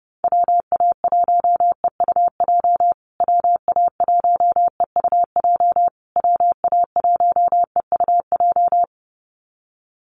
Morse code, heard as a keyed tone, about 28 wpm, 700 Hz. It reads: WA1EUJ WA1EUJ WA1EUJ
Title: USA Call Signs Repeated 3x 30wpm